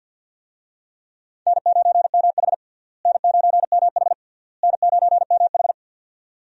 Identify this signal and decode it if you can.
Morse code, heard as a keyed tone, about 51 wpm, 700 Hz. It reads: N9MH N9MH N9MH
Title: USA Call Signs - 4 Characters - Repeated 3x - 50wpm